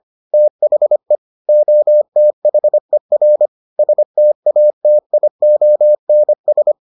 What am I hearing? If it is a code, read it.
Morse code, heard as a keyed tone, about 25 wpm, 600 Hz. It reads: THE OTHER STATIONS